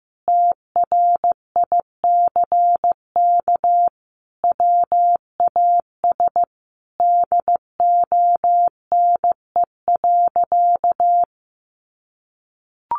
Morse code, heard as a keyed tone, about 15 wpm, 700 Hz. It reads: TRICK WAS DONE.
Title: Sentences from Top 1500 Words 15wpm